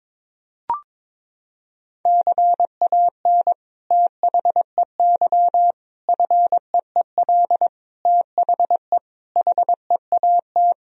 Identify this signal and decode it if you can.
Morse code, heard as a keyed tone, about 22 wpm, 700 Hz. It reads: CAN THEY FEEL THE HEAT
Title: Sentences from Top 1000 Words - Review Part 3 - 22wpm